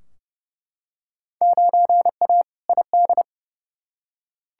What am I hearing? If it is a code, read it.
Morse code, heard as a keyed tone, about 30 wpm, 700 Hz. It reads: 9A ID
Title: Field Day Exchanges Encore - 30wpm